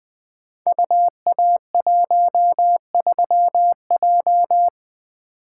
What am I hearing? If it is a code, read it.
Morse code, heard as a keyed tone, about 20 wpm, 700 Hz. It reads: UA13J
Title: Code Groups - 5 Letter:Numbers - 20wpm